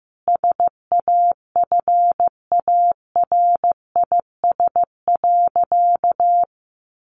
Morse code, heard as a keyed tone, about 15 wpm, 700 Hz. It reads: SAFARIS.
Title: A Sound Of Thunder 15wpm